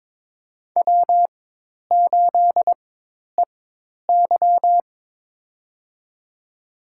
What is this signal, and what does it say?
Morse code, heard as a keyed tone, about 21 wpm, spaced as for 10 wpm, 700 Hz. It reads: W8EY
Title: USA Call Signs - 4x Character Spacing - 22wpm